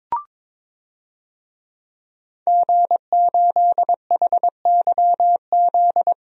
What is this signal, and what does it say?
Morse code, heard as a keyed tone, about 23 wpm, 700 Hz. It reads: G8HYZ
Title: DXCC-100 Call Signs - 5 Characters - 22wpm